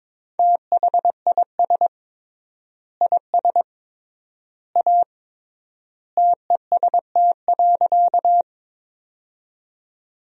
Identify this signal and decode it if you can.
Morse code, heard as a keyed tone, about 22 wpm, 700 Hz. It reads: THIS IS A TEST.
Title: 3x Word-Spacing - Sentences from Top 500 Words - 22wpm